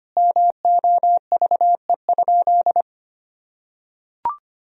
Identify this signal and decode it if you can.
Morse code, heard as a keyed tone, about 25 wpm, 700 Hz. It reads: MOVE?
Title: Sentences from Top 1000 Words - Review Part 1 - 25wpm